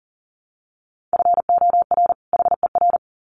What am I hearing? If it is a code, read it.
Morse code, heard as a keyed tone, about 40 wpm, 700 Hz. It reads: FOR HER
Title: Sets of 2 Words 40wpm